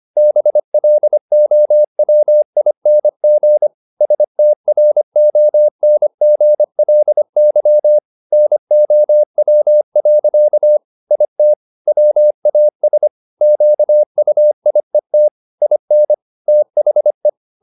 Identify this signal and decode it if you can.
Morse code, heard as a keyed tone, about 25 wpm, 600 Hz. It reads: BLOWING STRONGLY NOW. IT WAS QUIET IN THE